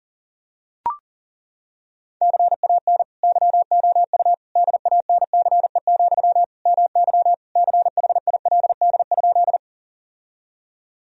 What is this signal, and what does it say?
Morse code, heard as a keyed tone, about 40 wpm, 700 Hz. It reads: CAN YOU DANCE, MY CHILD?
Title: Sentences from Top 1000 Words - Review Part 4 - 40wpm